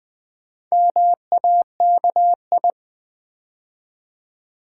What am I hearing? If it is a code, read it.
Morse code, heard as a keyed tone, about 20 wpm, 700 Hz. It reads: MAKI
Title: Build A Word - 2 Letters At A Time - 6-Letter Words - 20wpm